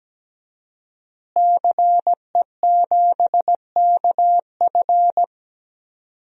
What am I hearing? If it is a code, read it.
Morse code, heard as a keyed tone, about 17 wpm, 700 Hz. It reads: CE7KF
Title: DXCC-100 Call Signs - 5 Characters - Repeated 3x - 17wpm